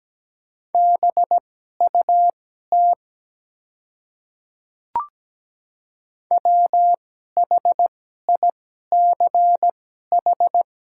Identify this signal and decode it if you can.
Morse code, heard as a keyed tone, about 17 wpm, spaced as for 12 wpm, 700 Hz. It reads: BUT WHICH
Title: 2x Character Spacing - Top 300 Words - 17wpm